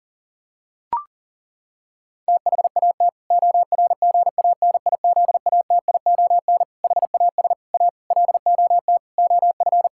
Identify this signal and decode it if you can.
Morse code, heard as a keyed tone, about 40 wpm, 700 Hz. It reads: THAT ORGANIZATION HAS A LOT OF
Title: Sentences from Top 1000 Words - Review Part 6 - 40wpm